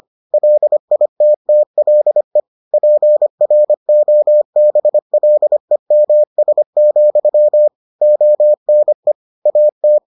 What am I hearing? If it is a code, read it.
Morse code, heard as a keyed tone, about 25 wpm, 600 Hz. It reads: LITTLE PROBLEMS, ONE AT